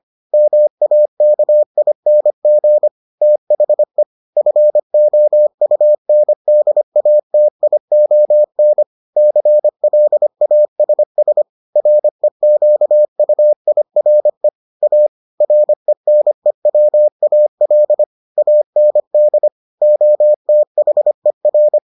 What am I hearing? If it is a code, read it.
Morse code, heard as a keyed tone, about 25 wpm, 600 Hz. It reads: MAKING THE FOUNDATION CLASS REQUIRE A RENEWAL AND OTHER